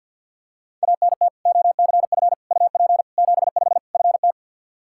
Morse code, heard as a keyed tone, about 50 wpm, 700 Hz. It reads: ANT OCF UP 75 FT
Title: Practice QSO - Exchange 2 - Info - 50wpm